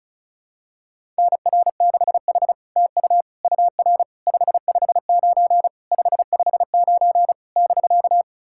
Morse code, heard as a keyed tone, about 36 wpm, 700 Hz. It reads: NR6H TU UR 559 559 <BK>
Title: SOTA QSO Practice 35wpm